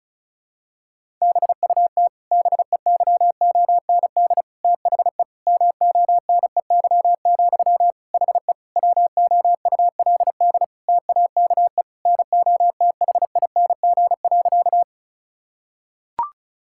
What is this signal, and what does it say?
Morse code, heard as a keyed tone, about 36 wpm, 700 Hz. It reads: BUT BEYOND THE MONEY, HE WOULD TAKE NOTHING.
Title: Sentences from Top 1000 Words - Review Part 6 - 35wpm